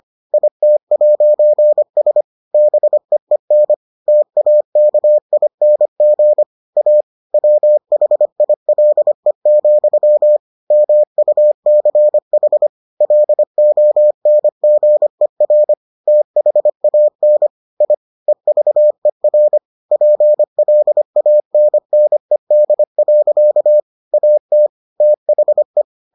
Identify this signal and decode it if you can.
Morse code, heard as a keyed tone, about 25 wpm, 600 Hz. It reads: IT'S BEEN TAKING A WHILE, MUCH LONGER THAN I EVER PLANNED. AT THE